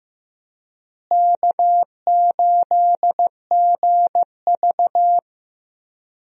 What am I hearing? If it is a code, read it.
Morse code, heard as a keyed tone, about 15 wpm, 700 Hz. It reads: K8GV
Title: USA Call Signs - 4 Characters - Repeated 2x - 15wpm